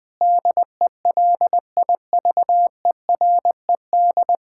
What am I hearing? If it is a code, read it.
Morse code, heard as a keyed tone, about 20 wpm, 700 Hz. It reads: DELIVERED
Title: Pangram Challenge v2 - 20wpm